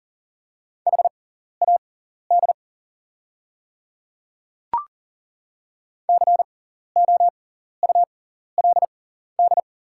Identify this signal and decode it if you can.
Morse code, heard as a keyed tone, about 40 wpm, spaced as for 14 wpm, 700 Hz. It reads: HAD COULD
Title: 6x Character Spacing - Top 100 Words - 40wpm